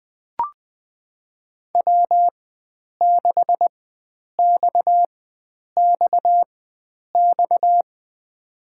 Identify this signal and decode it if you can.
Morse code, heard as a keyed tone, about 20 wpm, spaced as for 9 wpm, 700 Hz. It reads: W6XXX
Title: USA Call Signs - 5 Characters - 4x Character Spacing - 20wpm